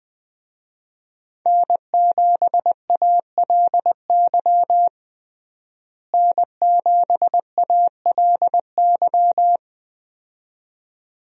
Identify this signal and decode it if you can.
Morse code, heard as a keyed tone, about 21 wpm, 700 Hz. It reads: N7ALY N7ALY
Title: USA Call Signs - 5 Characters - Repeated 2x - 20wpm